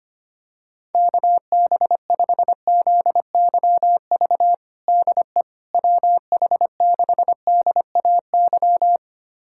Morse code, heard as a keyed tone, about 25 wpm, 700 Hz. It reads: KB5ZYV DE WH6DAY
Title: USA Callsign DE Callsign - 6 Characters - 25wpm